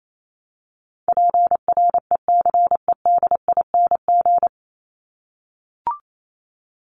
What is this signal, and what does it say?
Morse code, heard as a keyed tone, about 28 wpm, 700 Hz. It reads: PRECEDING
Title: ING Suffix - Top 100 Words with Base Words 28wpm